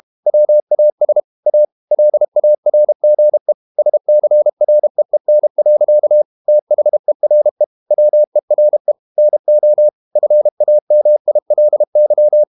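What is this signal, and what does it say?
Morse code, heard as a keyed tone, about 32 wpm, 600 Hz. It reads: WAS A LARGE SCREEN. THERE WERE NO FAMILY
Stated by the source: Morse code practice at 32 wpm from CW med Gnister, reading Ken Follett's novel Never in English.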